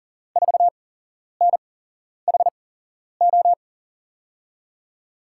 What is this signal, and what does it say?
Morse code, heard as a keyed tone, about 38 wpm, spaced as for 11 wpm, 700 Hz. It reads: 4NHO
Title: USA Call Signs - 5 Characters - 8x Character Spacing - 40wpm